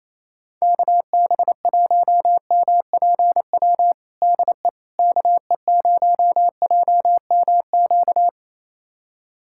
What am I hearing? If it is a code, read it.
Morse code, heard as a keyed tone, about 28 wpm, 700 Hz. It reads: KB1MPW DE KE0JMQ
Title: USA Callsign DE Callsign - 6 Characters - 28wpm